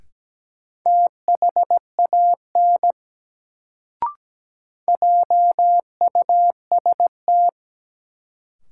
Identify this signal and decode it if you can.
Morse code, heard as a keyed tone, about 17 wpm, 700 Hz. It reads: THAN JUST
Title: Sets of 2 Words - New Word Drill - 17wpm